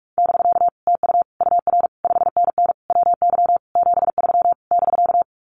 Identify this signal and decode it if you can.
Morse code, heard as a keyed tone, about 44 wpm, 700 Hz. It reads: <BK> TU UR 5NN WY 73 <BK>
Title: Common QSO Phrases Encore 45wpm